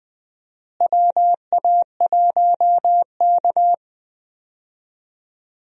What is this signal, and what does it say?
Morse code, heard as a keyed tone, about 20 wpm, 700 Hz. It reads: WA1K
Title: USA Call Signs - 4 Characters - Repeated 3x - 20wpm